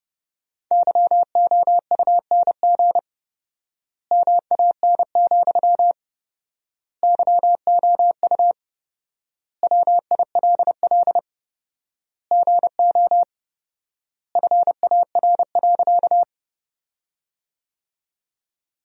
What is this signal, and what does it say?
Morse code, heard as a keyed tone, about 30 wpm, 700 Hz. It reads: YOUNG MAN, YOU WILL GO FAR.
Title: 4x Word-Spacing - Sentences from Top 400 Words - 30wpm